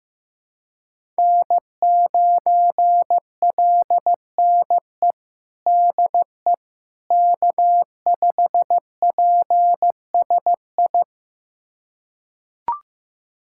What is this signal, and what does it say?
Morse code, heard as a keyed tone, about 15 wpm, 700 Hz. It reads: N9LNE DE K5PSI
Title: USA Callsign DE Callsign - 5 Characters - 15wpm